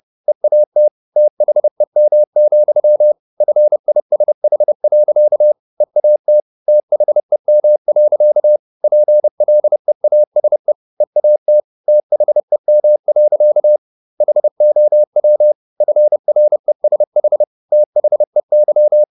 Morse code, heard as a keyed tone, about 30 wpm, 600 Hz. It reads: EAT THEM, FISH. EAT THEM. PLEASE EAT THEM. HOW FRESH THEY
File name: the old man and the sea / the old man and the sea 1 30wpm